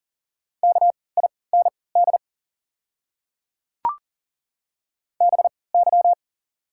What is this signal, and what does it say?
Morse code, heard as a keyed tone, about 39 wpm, spaced as for 22 wpm, 700 Hz. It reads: KIND BY
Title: 3x Character Spacing - Top 300 Words - 40wpm